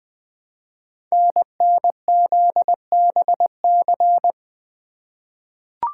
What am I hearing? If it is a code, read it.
Morse code, heard as a keyed tone, about 20 wpm, 700 Hz. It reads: NNZBC
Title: Code Groups - 5 Letters - 20wpm